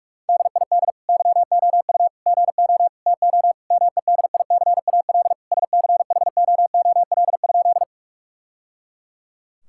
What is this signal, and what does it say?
Morse code, heard as a keyed tone, about 46 wpm, 700 Hz. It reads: DID YOU GO TO MEDICAL SCHOOL?
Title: Sentences from Top 1000 Words - Review Part 5 - 45wpm